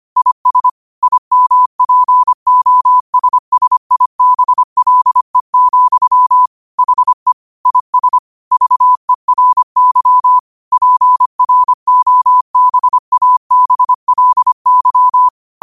Morse code, heard as a keyed tone, about 25 wpm, 1000 Hz. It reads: IS IMPOSSIBLE, HE IS VERY PROBABLY
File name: qotdfiles / 25wpm